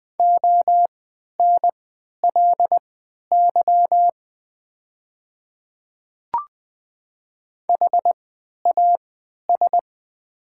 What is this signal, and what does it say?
Morse code, heard as a keyed tone, about 20 wpm, spaced as for 11 wpm, 700 Hz. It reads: ONLY HAS
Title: 3x Character Spacing - Top 200 Words - 20wpm